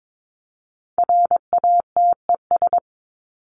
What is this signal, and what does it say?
Morse code, heard as a keyed tone, about 22 wpm, 700 Hz. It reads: RATES
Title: ES:S Suffix - Top 100 Words with Base Words 22wpm